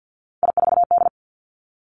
Morse code, heard as a keyed tone, about 50 wpm, 700 Hz. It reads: I4D
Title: Three Challenging Letter-Number Combinations 50wpm